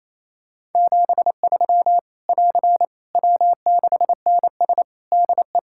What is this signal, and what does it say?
Morse code, heard as a keyed tone, about 28 wpm, 700 Hz. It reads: 73 <AR> W6NS DE
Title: Practice QSO - Exchange 4 - Ending - 28wpm